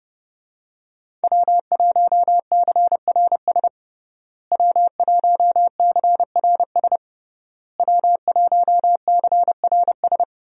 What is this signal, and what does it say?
Morse code, heard as a keyed tone, about 30 wpm, 700 Hz. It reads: W1CRS W1CRS W1CRS
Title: USA Call Signs - 5 Characters - Repeated 3x - 30wpm